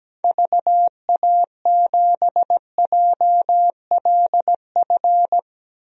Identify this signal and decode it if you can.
Morse code, heard as a keyed tone, about 17 wpm, 700 Hz. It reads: VA7JLF
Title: DXCC-100 Call Signs - 6 Characters - Repeated 2x - 17wpm